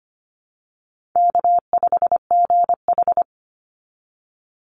Morse code, heard as a keyed tone, about 25 wpm, 700 Hz. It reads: K5GH
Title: Easy Call Signs - 25wpm